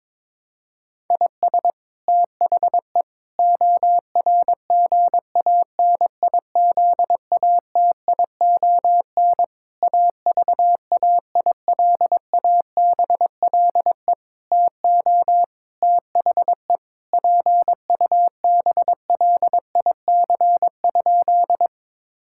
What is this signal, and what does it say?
Morse code, heard as a keyed tone, about 22 wpm, 700 Hz. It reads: IS THE ORGANIZATION AVAILABLE TO THE PUBLIC?
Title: Sentences from Top 1000 Words - Review Part 5 - 22wpm